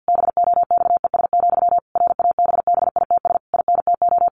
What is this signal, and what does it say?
Morse code, heard as a keyed tone, about 50 wpm, 700 Hz. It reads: BOXES, RABBITS INTO
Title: A Sound Of Thunder 50wpm